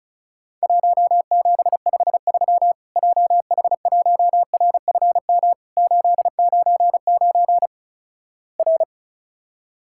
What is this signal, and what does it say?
Morse code, heard as a keyed tone, about 36 wpm, 700 Hz. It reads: 1753 JH1RFM 899 R
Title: Worked All Europe DX Contest - QTC Exchange - 35wpm